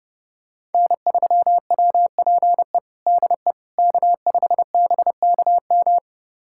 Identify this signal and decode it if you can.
Morse code, heard as a keyed tone, about 30 wpm, 700 Hz. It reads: N3WPE DE K5BKM
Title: USA Callsign DE Callsign - 5 Characters - 30wpm